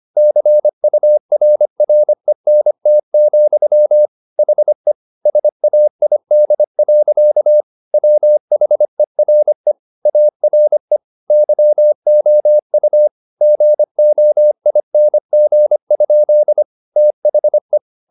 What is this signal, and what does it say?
Morse code, heard as a keyed tone, about 25 wpm, 600 Hz. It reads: CURRENT, HE SAID. WHERE ARE YOU GOING? THE
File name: the old man and the sea part 1 25wpm